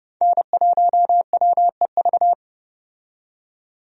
Morse code, heard as a keyed tone, about 31 wpm, 700 Hz. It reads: N1WEV
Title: USA Call Signs - 5 Characters - 30wpm